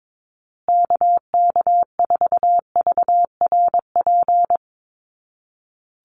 Morse code, heard as a keyed tone, about 22 wpm, 700 Hz. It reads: KK4VRP
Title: USA Call Signs Repeated 3x 22wpm